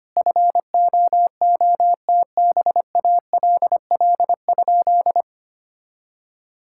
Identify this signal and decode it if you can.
Morse code, heard as a keyed tone, about 25 wpm, 700 Hz. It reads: FOOTBALL?
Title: Sentences from Top 1600 Words 25wpm